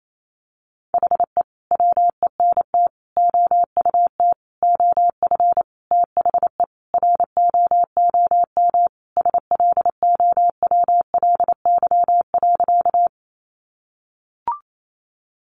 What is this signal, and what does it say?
Morse code, heard as a keyed tone, about 28 wpm, 700 Hz. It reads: HE WENT OUT OF THE ROOM SLOWLY.
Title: Sentences from Top 1100 Words 28wpm